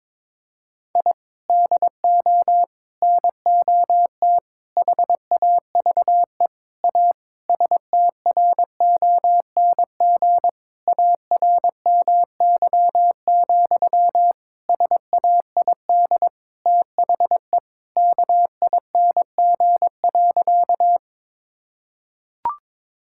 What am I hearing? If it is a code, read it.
Morse code, heard as a keyed tone, about 22 wpm, 700 Hz. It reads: I DO NOT HAVE A STRONG ARMY, SAID THE KING.
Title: Sentences from Top 1000 Words - Review Part 5 - 22wpm